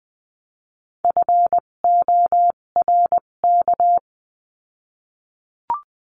Morse code, heard as a keyed tone, about 20 wpm, spaced as for 17 wpm, 700 Hz. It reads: FORK
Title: 257 Lesson 42b - Words 14-20wpm